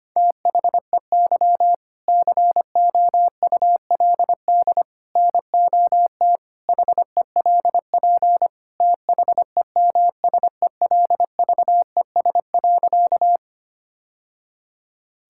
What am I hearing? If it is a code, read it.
Morse code, heard as a keyed tone, about 25 wpm, 700 Hz. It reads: THEY COULD NOT HELP THEMSELVES.
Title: Sentences from Top 1000 Words - Review Part 5 - 25wpm